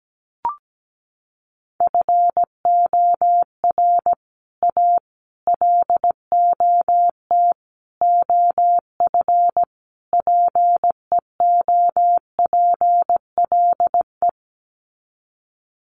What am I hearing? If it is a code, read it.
Morse code, heard as a keyed tone, about 17 wpm, 700 Hz. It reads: FOR A LOT OF PEOPLE
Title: Sets of 5 Words Encore 17wpm